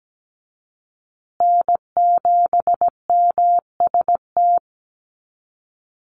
Morse code, heard as a keyed tone, about 17 wpm, 700 Hz. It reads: N7MST
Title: USA Call Signs Repeated 2x 17wpm